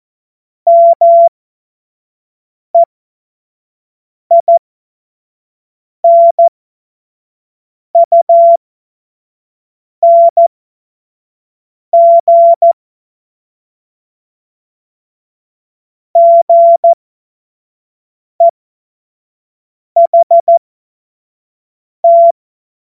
Morse code, heard as a keyed tone, about 14 wpm, spaced as for 5 wpm, 680 Hz. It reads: MEINUNG GEHT
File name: Morsekurs / Witze / Witz 13. Juni 2023 25 F70